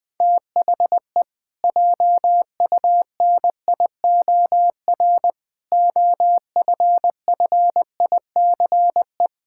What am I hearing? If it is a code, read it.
Morse code, heard as a keyed tone, about 20 wpm, 700 Hz. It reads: THE JUNIOR OFFICE
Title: Pangram Challenge v2 - 20wpm